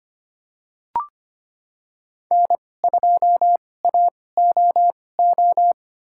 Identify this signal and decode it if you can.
Morse code, heard as a keyed tone, about 25 wpm, spaced as for 18 wpm, 700 Hz. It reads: N2AOO
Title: USA Call Signs - 5 Characters - 2x Character Spacing - 25wpm